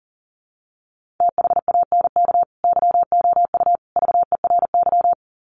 Morse code, heard as a keyed tone, about 40 wpm, 700 Hz. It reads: THANK YOU VERY
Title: Sets of 3 Words 40wpm